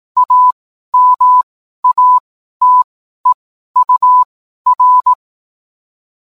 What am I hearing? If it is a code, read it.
Morse code, heard as a keyed tone, about 18 wpm, spaced as for 12 wpm, 1000 Hz. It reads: AMATEUR